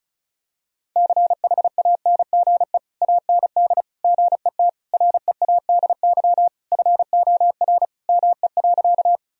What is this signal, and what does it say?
Morse code, heard as a keyed tone, about 34 wpm, 700 Hz. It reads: CHANGE AND GET READY FOR ME.
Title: Sentences from Top 1000 Words - Review Part 1 - 35wpm